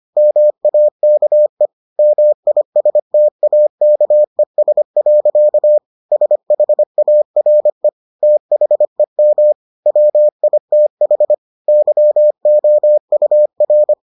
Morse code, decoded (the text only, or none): MAKE MISTAKES. SHARE THEM WITH YOUR